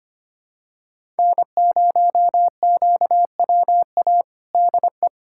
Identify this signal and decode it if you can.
Morse code, heard as a keyed tone, about 25 wpm, 700 Hz. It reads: N0QWA DE
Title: USA Callsign DE Callsign - 5 Characters - 25wpm